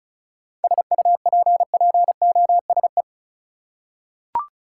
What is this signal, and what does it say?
Morse code, heard as a keyed tone, about 36 wpm, 700 Hz. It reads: SUPPOSE
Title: Seven Letter Words - 35wpm